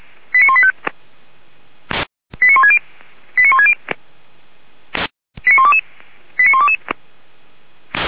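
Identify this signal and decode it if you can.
Five-tone selective call (ZVEI), heard as quick runs of five tones, tones of about 70 ms. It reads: 89185 89160 89160 89122 89122
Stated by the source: Five-tone ZVEI selective call received by SDR, from the Feuerwehr Orlamünde (fire brigade) audio server.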